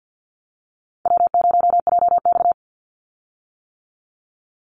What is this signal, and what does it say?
Morse code, heard as a keyed tone, about 48 wpm, 700 Hz. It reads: W0JX
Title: Call Signs - 50wpm